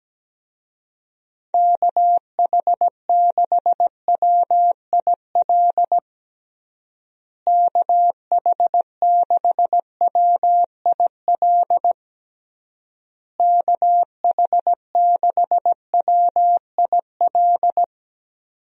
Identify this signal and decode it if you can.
Morse code, heard as a keyed tone, about 17 wpm, 700 Hz. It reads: KH6WIL KH6WIL KH6WIL
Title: USA Call Signs - 6 Characters - Repeated 3x - 17wpm